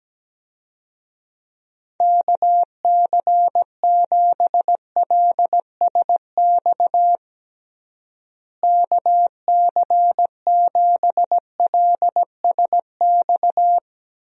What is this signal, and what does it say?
Morse code, heard as a keyed tone, about 17 wpm, 700 Hz. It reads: KC7LSX KC7LSX
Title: USA Call Signs - 6 Characters - Repeated 3x - 17wpm